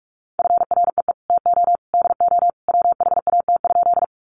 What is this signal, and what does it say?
Morse code, heard as a keyed tone, about 44 wpm, 700 Hz. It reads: FREE TO DO WHAT?
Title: Sentences from Top 500 Words 45wpm